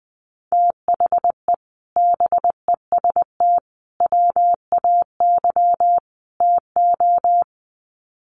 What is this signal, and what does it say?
Morse code, heard as a keyed tone, about 20 wpm, 700 Hz. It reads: THE BEST WAY TO
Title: Sets of 4 Words 20wpm